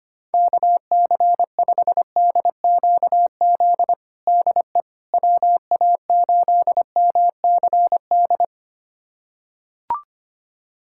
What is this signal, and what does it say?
Morse code, heard as a keyed tone, about 25 wpm, 700 Hz. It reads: KC5DQZ DE WA8MCD
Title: USA Callsign DE Callsign 25wpm